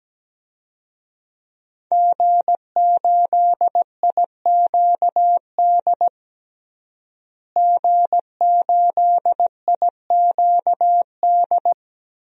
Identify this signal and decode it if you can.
Morse code, heard as a keyed tone, about 17 wpm, 700 Hz. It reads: G8IQD G8IQD
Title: DXCC-100 Call Signs - 5 Characters - Repeated 2x - 17wpm